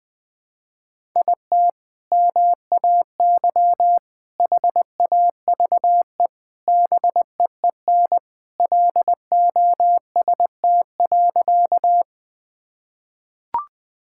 Morse code, heard as a keyed tone, about 21 wpm, 700 Hz. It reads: IT MAY HAVE BEEN LOST.
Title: Sentences with 3-Word Phrases - 20wpm